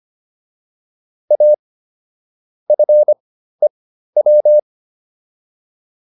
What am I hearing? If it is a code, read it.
Morse code, heard as a keyed tone, about 25 wpm, spaced as for 13 wpm, 600 Hz. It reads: A FEW